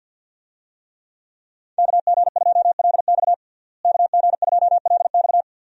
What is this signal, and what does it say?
Morse code, heard as a keyed tone, about 52 wpm, 700 Hz. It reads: KG2LX KG2LX
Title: USA Call Signs - 5 Characters - Repeated 3x - 50wpm